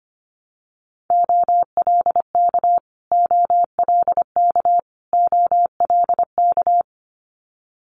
Morse code, heard as a keyed tone, about 25 wpm, 700 Hz. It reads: OLK OLK OLK
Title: Multi-letter Phonemes Repeated 3x 25wpm